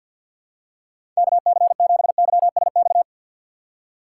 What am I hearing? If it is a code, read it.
Morse code, heard as a keyed tone, about 52 wpm, 700 Hz. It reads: KC7YSX
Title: USA Call Signs - 2x Character Spacing - 50wpm